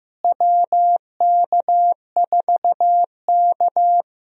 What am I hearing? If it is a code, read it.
Morse code, heard as a keyed tone, about 15 wpm, 700 Hz. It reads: WK4K
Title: USA Call Signs - 4 Characters - Repeated 3x - 15wpm